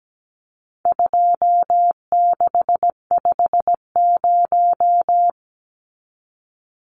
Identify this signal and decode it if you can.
Morse code, heard as a keyed tone, about 17 wpm, 700 Hz. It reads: 2650
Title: Numbers - 4-Digit 17wpm